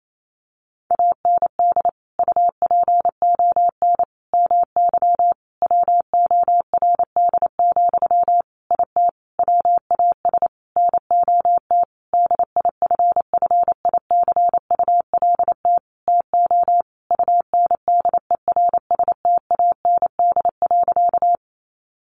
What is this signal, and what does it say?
Morse code, heard as a keyed tone, about 28 wpm, 700 Hz. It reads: AND UPON MY WORD, IT WAS NOT DIFFICULT TO UNDERSTAND.
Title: Sentences from Top 1100 Words 28wpm